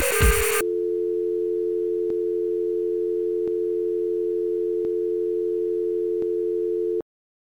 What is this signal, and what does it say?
Dial tone (North America), 350 + 440 Hz.